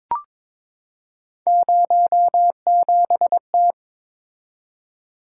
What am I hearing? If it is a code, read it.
Morse code, heard as a keyed tone, about 23 wpm, 700 Hz. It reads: 07T
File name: practice / 22wpm / Three Letter-Number Combinations - 22wpm